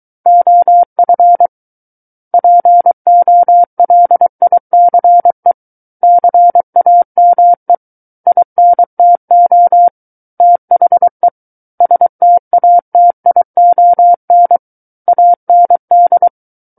Morse code, heard as a keyed tone, about 23 wpm, 700 Hz. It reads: OF POLICE CAME INTO THE STATION AND
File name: Book One / 14-Chapter 14 - 23 WPM